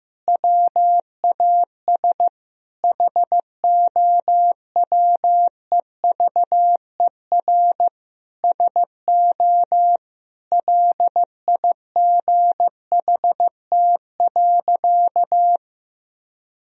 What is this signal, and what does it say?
Morse code, heard as a keyed tone, about 15 wpm, 700 Hz. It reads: WAS HOWEVER SO LIGHT.
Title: Sentences from Top 1000 Words - Review Part 2 - 15wpm